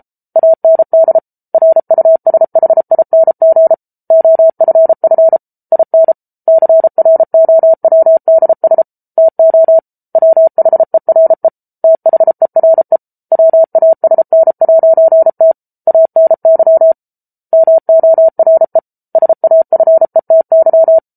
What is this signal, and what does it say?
Morse code, heard as a keyed tone, about 34 wpm, 650 Hz. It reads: AND RUSHING OFF IN CROWDS TO WHERE THERE WASN'T ANY MORE SAFETY